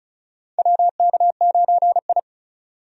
Morse code, heard as a keyed tone, about 36 wpm, 700 Hz. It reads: WK9I
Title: USA Call Signs - 4 Characters - 35wpm